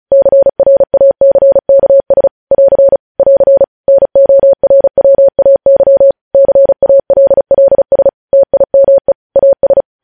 Morse code, heard as a keyed tone, about 36 wpm, 550 Hz. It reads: CRACKS <AR> <AR> NORWAY CALLS TIME AS
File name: podcast / News headlines-1x-35x35